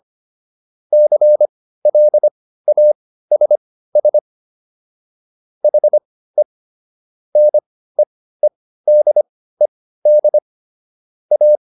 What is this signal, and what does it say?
Morse code, heard as a keyed tone, about 25 wpm, spaced as for 15 wpm, 600 Hz. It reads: CLASS HE NEEDED A